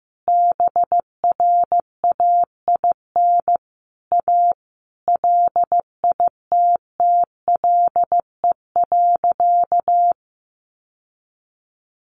Morse code, heard as a keyed tone, about 15 wpm, 700 Hz. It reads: BRAIN A LITTLE.
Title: Sentences from Top 1500 Words 15wpm